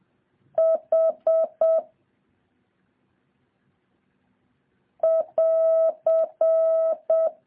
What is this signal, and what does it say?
Morse code, heard as a keyed tone, about 7 wpm, 650 Hz. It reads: H <AR>
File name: jh8cbh / 2023cw